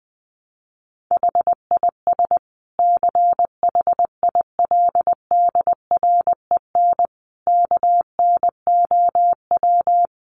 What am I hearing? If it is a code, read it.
Morse code, heard as a keyed tone, about 20 wpm, 700 Hz. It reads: HIS CHILDREN KNOW
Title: Sentences from Top 900 Words 20wpm